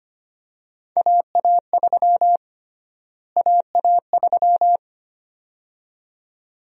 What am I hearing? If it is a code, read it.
Morse code, heard as a keyed tone, about 25 wpm, 700 Hz. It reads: AA3 AA3
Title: USA Call Sign Prefixes - Repeated 2x - 25wpm